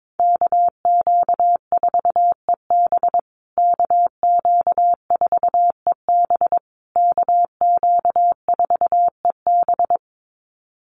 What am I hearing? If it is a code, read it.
Morse code, heard as a keyed tone, about 22 wpm, 700 Hz. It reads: KQ4EB KQ4EB KQ4EB
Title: USA Call Signs Repeated 3x 22wpm